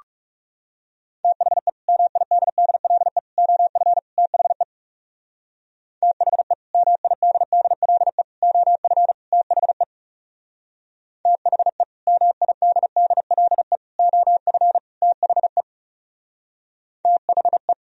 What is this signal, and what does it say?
Morse code, heard as a keyed tone, about 37 wpm, 700 Hz. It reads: THE MIDDLE OF THE THE MIDDLE OF THE THE MIDDLE OF THE THE
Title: Speed-Racing - Sets of 4 Words 30wpm